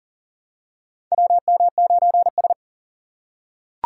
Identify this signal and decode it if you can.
Morse code, heard as a keyed tone, about 40 wpm, 700 Hz. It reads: WM9S
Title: USA Call Signs - 4 Characters - 3x Character Spacing - 40wpm